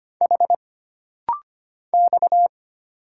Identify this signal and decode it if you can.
Morse code, heard as a keyed tone, about 25 wpm, 700 Hz. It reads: H X
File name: Single Character (Letter:Number:Punctuation:Prosign) - Rapid-Fire - 25wpm